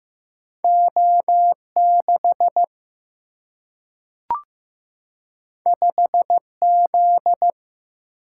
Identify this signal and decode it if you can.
Morse code, heard as a keyed tone, about 15 wpm, 700 Hz. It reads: O6 5Z
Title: Single-Letter Single-Number 15wpm